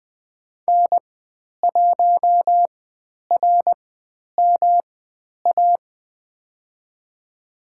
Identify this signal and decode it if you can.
Morse code, heard as a keyed tone, about 20 wpm, spaced as for 10 wpm, 700 Hz. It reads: N1RMA
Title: 109 Lesson 18g - Letter m - Call Signs 20wpm